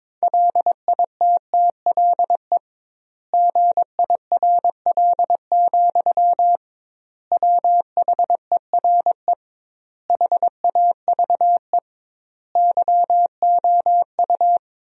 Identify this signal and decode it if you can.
Morse code, heard as a keyed tone, about 22 wpm, 700 Hz. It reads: LITTLE GIRL, WHERE HAVE YOU
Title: 2x Word-Spacing - Sentences from Top 400 Words - 22wpm